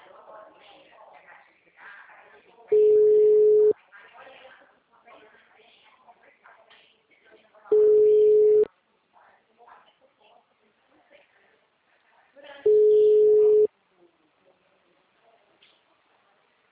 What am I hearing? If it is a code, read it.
Ringback tone (Europe, 425 Hz), with speech.
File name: out- -2009- - - .55244